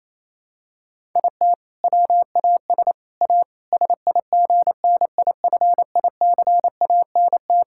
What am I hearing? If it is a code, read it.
Morse code, heard as a keyed tone, about 28 wpm, 700 Hz. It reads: IT WAS A SIGNIFICANT
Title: Sentences from Top 1000 Words - Review Part 6 - 28wpm